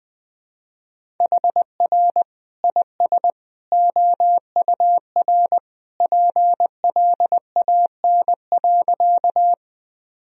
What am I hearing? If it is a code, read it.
Morse code, heard as a keyed tone, about 21 wpm, 700 Hz. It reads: HR IS OUR PLAN.
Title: Sentences from Top 500 Words and Abbreviations 20wpm